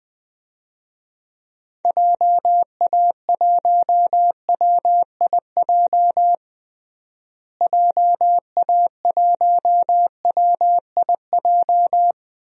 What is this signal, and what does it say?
Morse code, heard as a keyed tone, about 20 wpm, 700 Hz. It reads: JA1WIJ JA1WIJ
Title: DXCC-100 Call Signs - 6 Characters - Repeated 2x - 20wpm